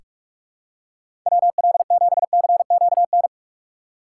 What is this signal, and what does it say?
Morse code, heard as a keyed tone, about 44 wpm, 700 Hz. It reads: WL7CQN
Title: USA Call Signs - 6 Characters - 2x Character Spacing - 45wpm